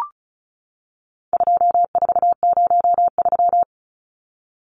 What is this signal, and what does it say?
Morse code, heard as a keyed tone, about 34 wpm, 700 Hz. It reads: 2403
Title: Numbers - 4-Digit 35wpm